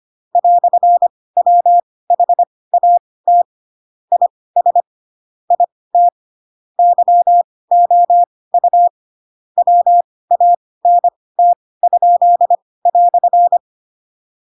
Morse code, heard as a keyed tone, about 25 wpm, spaced as for 18 wpm, 700 Hz. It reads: "WHAT IS IT YOU WANT?"
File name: Pinocchio / Pinocchio ch2 12 25 ik1hge